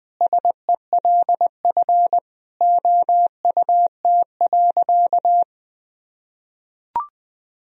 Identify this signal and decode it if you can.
Morse code, heard as a keyed tone, about 21 wpm, 700 Hz. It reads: SELF OUT.
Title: Sentences from Top 1000 Words - Review Part 1 - 20wpm